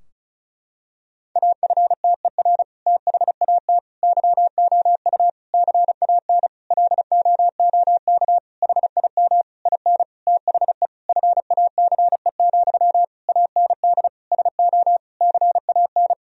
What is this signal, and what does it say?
Morse code, heard as a keyed tone, about 34 wpm, 700 Hz. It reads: AFTER THAT YOU CAN LOOK HIM IN THE FACE, AND SO CAN
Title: Sentences from Top 1000 Words - Review Part 2 - 35wpm